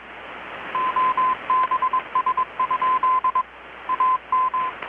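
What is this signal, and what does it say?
Morse code, heard as a keyed tone, about 22 wpm, 1030 Hz. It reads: OBS? AM
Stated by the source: Morse from KPH's Night of Nights XX maritime radiotelegraphy event, received in Chiapas, Mexico about 3,800 km away on a Kenwood R-600 with a grounded 10 m long-wire antenna.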